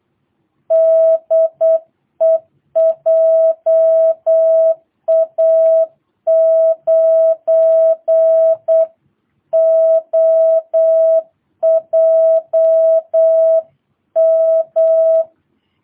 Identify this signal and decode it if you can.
Morse code, heard as a keyed tone, about 8 wpm, 660 Hz. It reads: DE JA9OJM
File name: jh8cbh / 2022cw